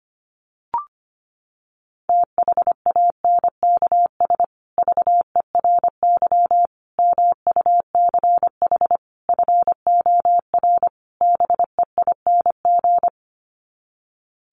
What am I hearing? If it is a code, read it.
Morse code, heard as a keyed tone, about 25 wpm, 700 Hz. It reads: THANKS VERY MUCH FOR BEING
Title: Sets of 5 Words Encore 25wpm